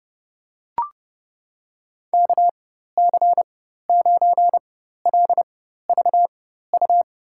Morse code, heard as a keyed tone, about 30 wpm, spaced as for 14 wpm, 700 Hz. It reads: KC9LVU
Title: USA Call Signs - 4x Character Spacing - 30wpm